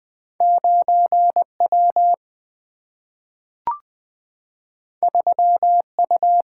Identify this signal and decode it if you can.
Morse code, heard as a keyed tone, about 20 wpm, 700 Hz. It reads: 9W 3U
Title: Single-Letter Single-Number 20wpm